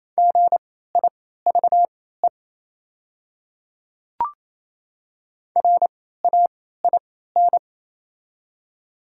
Morse code, heard as a keyed tone, about 28 wpm, spaced as for 16 wpm, 700 Hz. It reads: GIVE RAIN
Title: 3x Character Spacing - Top 300 Words - 28wpm